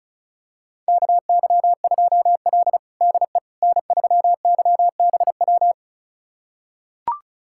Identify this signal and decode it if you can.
Morse code, heard as a keyed tone, about 34 wpm, 700 Hz. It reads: KY2L DE N3YBW
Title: USA Callsign DE Callsign 35wpm